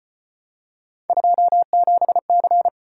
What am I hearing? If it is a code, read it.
Morse code, heard as a keyed tone, about 34 wpm, 700 Hz. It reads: 27C